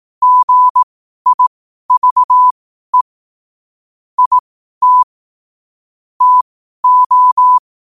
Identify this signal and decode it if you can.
Morse code, heard as a keyed tone, about 18 wpm, spaced as for 12 wpm, 1000 Hz. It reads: GIVE IT TO
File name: qotdfiles / 12wpm